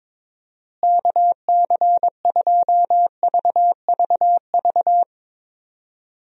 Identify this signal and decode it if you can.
Morse code, heard as a keyed tone, about 21 wpm, 700 Hz. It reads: KC2VVV
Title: USA Call Signs - 6 Characters - Repeated 2x - 22wpm